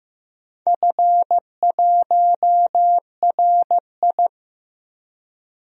Morse code, heard as a keyed tone, about 15 wpm, 700 Hz. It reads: F1RI
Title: DXCC-100 Call Signs - 4 Characters - 15wpm